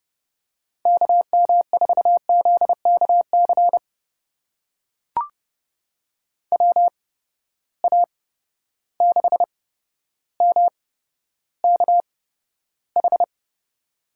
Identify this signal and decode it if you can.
Morse code, heard as a keyed tone, about 31 wpm, spaced as for 13 wpm, 700 Hz. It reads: KM4ZKC WA6MKH
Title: USA Call Signs - 6 Characters - 8x Character Spacing - 30wpm